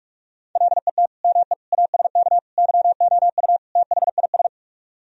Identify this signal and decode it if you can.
Morse code, heard as a keyed tone, about 49 wpm, 700 Hz. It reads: LET ME ASK YOU THIS